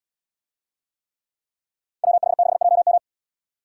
Morse code, heard as a keyed tone, about 78 wpm, 700 Hz. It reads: WH6JM